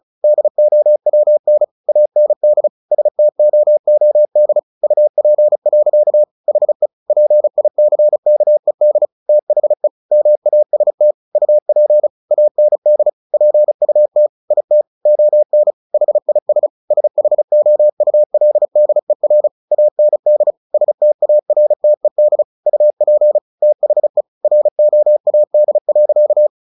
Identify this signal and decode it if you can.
Morse code, heard as a keyed tone, about 34 wpm, 600 Hz. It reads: DOWN AND STOOD UP. HE PICKED THE MAST UP AND PUT IT ON HIS SHOULDER AND STARTED UP THE ROAD.
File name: the old man and the sea 2 35wpm